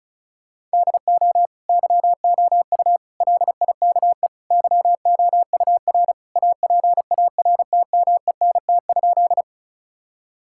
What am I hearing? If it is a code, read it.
Morse code, heard as a keyed tone, about 36 wpm, 700 Hz. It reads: DO YOU LIKE YOUR APARTMENT?
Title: Sentences from Top 1000 Words - Review Part 6 - 35wpm